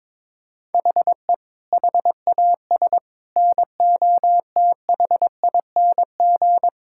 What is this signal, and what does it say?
Morse code, heard as a keyed tone, about 22 wpm, 700 Hz. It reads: HE HAS NOTHING
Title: Sentences from Top 1000 Words - Review Part 5 - 22wpm